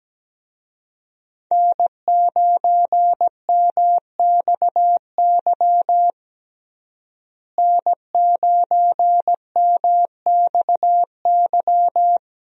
Morse code, heard as a keyed tone, about 17 wpm, 700 Hz. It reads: N9MXY N9MXY
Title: USA Call Signs - 5 Characters - Repeated 3x - 17wpm